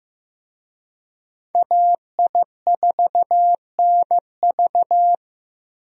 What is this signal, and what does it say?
Morse code, heard as a keyed tone, about 16 wpm, 700 Hz. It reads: AI4NV
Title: USA Call Signs - 5 Characters - Repeated 2x - 15wpm